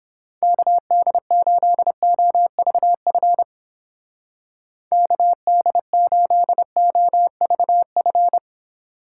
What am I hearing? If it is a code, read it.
Morse code, heard as a keyed tone, about 29 wpm, 700 Hz. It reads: KD8OVF KD8OVF